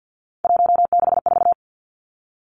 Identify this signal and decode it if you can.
Morse code, heard as a keyed tone, about 50 wpm, 700 Hz. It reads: 164
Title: Three Challenging Letter-Number Combinations 50wpm